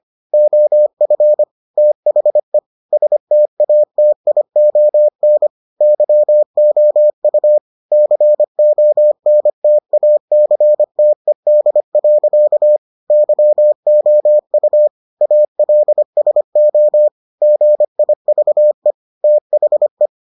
Morse code, heard as a keyed tone, about 25 wpm, 600 Hz. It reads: OF THE STATION YOU CONTACTED. YOU ALSO GIVE THE